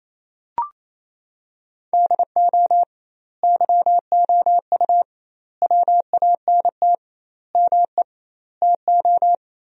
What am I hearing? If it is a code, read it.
Morse code, heard as a keyed tone, about 27 wpm, 700 Hz. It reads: DO YOU WANT ME TO
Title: 2x Word-Spacing - Sentences from Top 200 Words - 28wpm